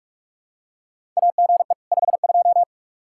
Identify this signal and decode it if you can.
Morse code, heard as a keyed tone, about 44 wpm, 700 Hz. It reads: AGE 52
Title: Practice QSO - Exchange 3 - More Info - 45wpm